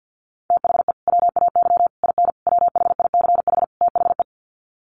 Morse code, heard as a keyed tone, about 50 wpm, 700 Hz. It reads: THE WAY IN WHICH THE
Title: Sets of 5 Words Encore2 50wpm